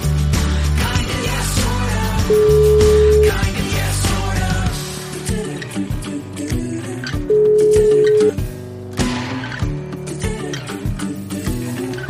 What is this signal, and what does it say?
Ringback tone (Europe, 425 Hz), with noise.